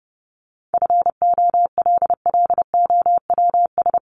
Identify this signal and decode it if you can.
Morse code, heard as a keyed tone, about 30 wpm, 700 Hz. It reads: FOLLOWS
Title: ES:S Suffix - Top 100 Words with Base Words 30wpm